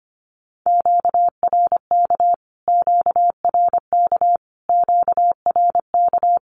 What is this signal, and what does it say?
Morse code, heard as a keyed tone, about 25 wpm, 700 Hz. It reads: QRK QRK QRK
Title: Q-Codes Repeated 3x 25wpm